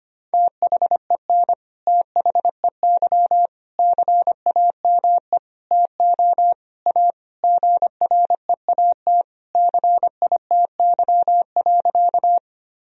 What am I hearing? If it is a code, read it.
Morse code, heard as a keyed tone, about 25 wpm, 700 Hz. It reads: THEN THEY CAME TO A GREAT CITY.
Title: Sentences from Top 1000 Words - Review Part 1 - 25wpm